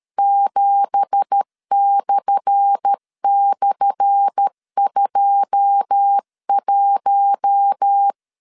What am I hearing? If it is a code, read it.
Morse code, heard as a keyed tone, about 13 wpm, 800 Hz. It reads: 7//21
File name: LEZIONI / 10 SIGNS / 10A SLASH 12wpm 2314